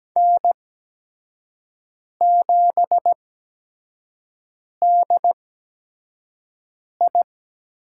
Morse code, heard as a keyed tone, about 17 wpm, spaced as for 5 wpm, 700 Hz. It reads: N7DI
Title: USA Call Signs - 5 Characters - 8x Character Spacing - 17wpm